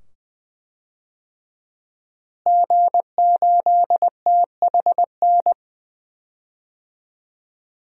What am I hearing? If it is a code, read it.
Morse code, heard as a keyed tone, about 20 wpm, 700 Hz. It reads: G8THN
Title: DXCC-100 Call Signs - 5 Characters - Repeated 2x - 20wpm